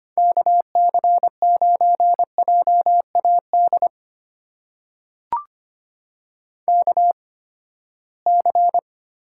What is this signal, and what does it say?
Morse code, heard as a keyed tone, about 25 wpm, spaced as for 17 wpm, 700 Hz. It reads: KC9JAD KC
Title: USA Call Signs - 6 Characters - 8x Character Spacing - 25wpm